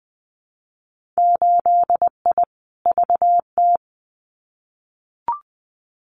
Morse code, heard as a keyed tone, about 20 wpm, 700 Hz. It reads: 8I VT
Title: Winter Field Day Exchanges Encore 20wpm